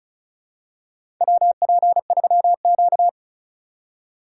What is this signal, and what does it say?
Morse code, heard as a keyed tone, about 34 wpm, 700 Hz. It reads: WP3Q
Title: USA Call Signs - 4 Characters - 4x Character Spacing - 35wpm